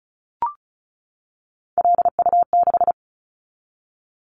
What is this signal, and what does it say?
Morse code, heard as a keyed tone, about 36 wpm, 700 Hz. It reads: LU6
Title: Callsign Prefixes for Top 100 DXCC - 35wpm